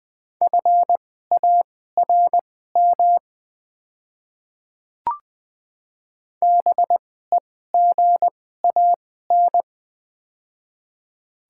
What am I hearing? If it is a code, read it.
Morse code, heard as a keyed tone, about 20 wpm, spaced as for 15 wpm, 700 Hz. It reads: FARM BEGAN
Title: 2x Character Spacing - Top 500 Words - 20wpm